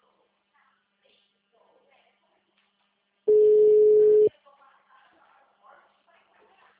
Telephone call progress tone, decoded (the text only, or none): ringback tone